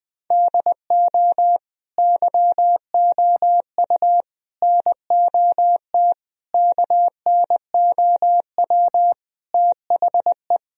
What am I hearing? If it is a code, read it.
Morse code, heard as a keyed tone, about 20 wpm, 700 Hz. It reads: DO YOU NOT KNOW THE
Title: Sentences from Top 1000 Words - Review Part 3 - 20wpm